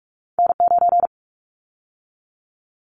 Morse code, heard as a keyed tone, about 46 wpm, 700 Hz. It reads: N9
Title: Callsign Prefixes for Top 25 DXCC - 45wpm